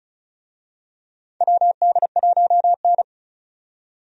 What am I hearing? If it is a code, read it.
Morse code, heard as a keyed tone, about 36 wpm, 700 Hz. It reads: WD1N
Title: USA Call Signs - 4 Characters - 35wpm